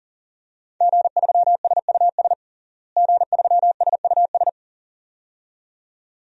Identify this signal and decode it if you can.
Morse code, heard as a keyed tone, about 40 wpm, 700 Hz. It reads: G3SUS G3SUS
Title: DXCC-100 Call Signs - 5 Characters - Repeated 2x - 40wpm